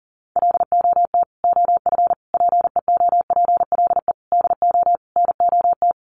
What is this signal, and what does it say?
Morse code, heard as a keyed tone, about 40 wpm, 700 Hz. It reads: LOT OF PEOPLE DO NOT
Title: Sets of 5 Words Encore 40wpm